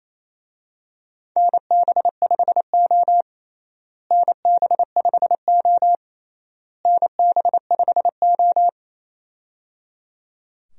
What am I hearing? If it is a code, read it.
Morse code, heard as a keyed tone, about 28 wpm, 700 Hz. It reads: NB5O NB5O NB5O
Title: USA Call Signs - 4 Characters - Repeated 3x - 28wpm